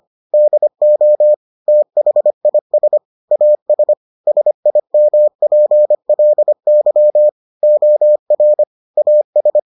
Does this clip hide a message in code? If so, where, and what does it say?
Yes: Morse code from 0.3 to 9.6 s, heard as a keyed tone, about 25 wpm, 600 Hz. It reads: DO THIS AS SIMPLY OR AS